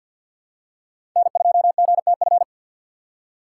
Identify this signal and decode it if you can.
Morse code, heard as a keyed tone, about 49 wpm, 700 Hz. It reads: N2GTF
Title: USA Call Signs - 5 Characters - 4x Character Spacing - 50wpm